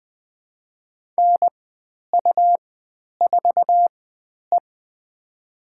Morse code, heard as a keyed tone, about 20 wpm, spaced as for 10 wpm, 700 Hz. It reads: NU4E
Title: 109 Lesson 18g - Letter m - Call Signs 20wpm